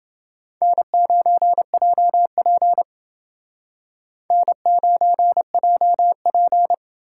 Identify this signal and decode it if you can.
Morse code, heard as a keyed tone, about 29 wpm, 700 Hz. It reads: N9JP N9JP